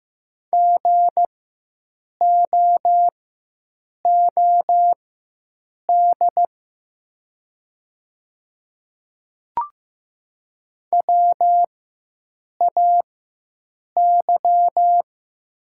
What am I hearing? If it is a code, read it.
Morse code, heard as a keyed tone, about 15 wpm, spaced as for 7 wpm, 700 Hz. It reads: GOOD WAY
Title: 4x Character Spacing - Top 200 Words - 15wpm